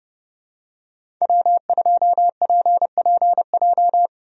Morse code, heard as a keyed tone, about 31 wpm, 700 Hz. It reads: W2PPJ
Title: USA Call Signs - 5 Characters - 30wpm